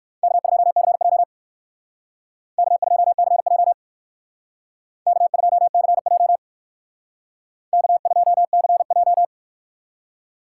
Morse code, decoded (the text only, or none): K2CJ K2CJ K2CJ K2CJ